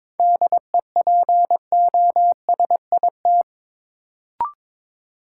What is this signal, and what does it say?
Morse code, heard as a keyed tone, about 22 wpm, 700 Hz. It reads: DEPOSIT
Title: Sentences from Top 1300 Words - New Word Drill - 22wpm